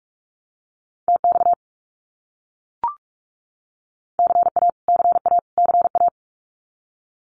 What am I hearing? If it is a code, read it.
Morse code, heard as a keyed tone, about 44 wpm, 700 Hz. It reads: TX CA CA CA
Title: US States Repeated 3x 45wpm